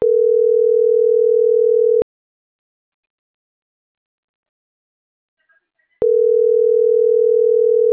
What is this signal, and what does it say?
Ringback tone (North America), 440 + 480 Hz.